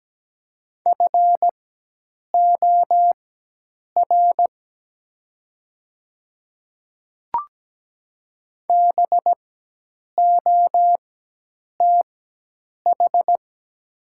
Morse code, heard as a keyed tone, about 17 wpm, spaced as for 8 wpm, 700 Hz. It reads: FOR BOTH
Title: 4x Character Spacing - Top 300 Words - 17wpm